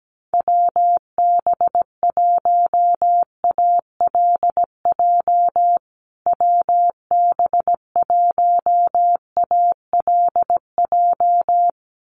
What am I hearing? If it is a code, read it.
Morse code, heard as a keyed tone, about 17 wpm, 700 Hz. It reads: WB1ALJ WB1ALJ
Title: USA Call Signs Repeated 2x 17wpm